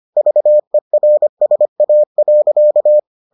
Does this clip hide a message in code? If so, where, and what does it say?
Yes: Morse code from 0.2 to 3.0 s, heard as a keyed tone, about 25 wpm, 600 Hz. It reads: VERSA.